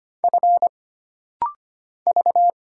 Morse code, heard as a keyed tone, about 25 wpm, 700 Hz. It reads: F V
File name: Single Character (Letter:Number:Punctuation:Prosign) - Rapid-Fire - 25wpm